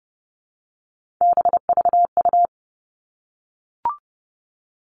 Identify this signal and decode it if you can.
Morse code, heard as a keyed tone, about 30 wpm, 700 Hz. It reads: BVU
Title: All Cut Numbers - 3-Digit 30wpm